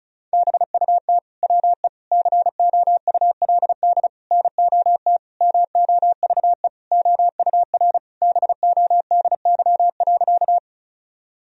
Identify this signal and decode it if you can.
Morse code, heard as a keyed tone, about 34 wpm, 700 Hz. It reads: BUT WE COULD NOT MOVE OUR BODY.
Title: Sentences from Top 1000 Words - Review Part 2 - 35wpm